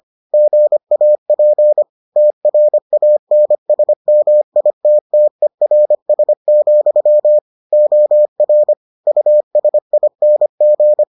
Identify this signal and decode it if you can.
Morse code, heard as a keyed tone, about 25 wpm, 600 Hz. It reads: GAP TRANSMITTERS, OR USING